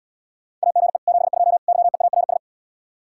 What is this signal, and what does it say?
Morse code, heard as a keyed tone, about 80 wpm, 700 Hz. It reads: AGE 72 YEARS